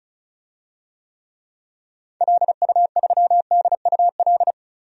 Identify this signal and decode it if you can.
Morse code, heard as a keyed tone, about 34 wpm, 700 Hz. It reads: LU3DUL
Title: DXCC-100 Call Signs - 6 Characters - 35wpm